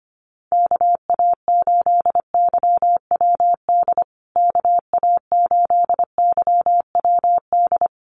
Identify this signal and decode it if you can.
Morse code, heard as a keyed tone, about 25 wpm, 700 Hz. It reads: KA8YWD KA8YWD
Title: USA Call Signs Repeated 2x 25wpm